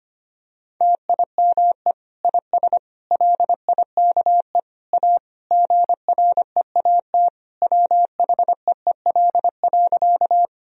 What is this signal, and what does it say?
Morse code, heard as a keyed tone, about 25 wpm, 700 Hz. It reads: TIME IS LIKE A GREAT WHEEL.
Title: Sentences from Top 1000 Words - Review Part 3 - 25wpm